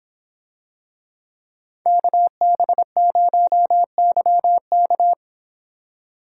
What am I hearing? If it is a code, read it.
Morse code, heard as a keyed tone, about 25 wpm, 700 Hz. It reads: KB0YK